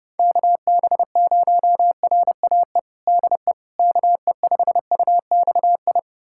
Morse code, heard as a keyed tone, about 30 wpm, 700 Hz. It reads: KB0RAE DE KE5UXI
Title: USA Callsign DE Callsign 30wpm